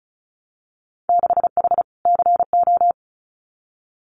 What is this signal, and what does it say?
Morse code, heard as a keyed tone, about 36 wpm, 700 Hz. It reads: 6H CO
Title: Winter Field Day Exchanges Encore 35wpm